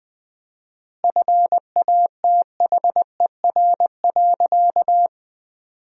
Morse code, heard as a keyed tone, about 20 wpm, 700 Hz. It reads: FATHER.
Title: 4x Word-Spacing - Sentences from Top 300 Words - 20wpm